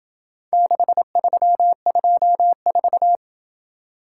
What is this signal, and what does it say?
Morse code, heard as a keyed tone, about 27 wpm, 700 Hz. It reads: 6324